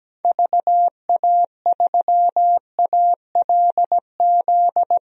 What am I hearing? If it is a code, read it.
Morse code, heard as a keyed tone, about 17 wpm, 700 Hz. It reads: VA3ALZ
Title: DXCC-100 Call Signs - 6 Characters - 17wpm